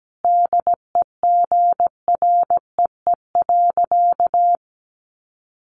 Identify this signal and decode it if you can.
Morse code, heard as a keyed tone, about 17 wpm, 700 Hz. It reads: DEGREE.
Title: Sentences from Top 1500 Words 17wpm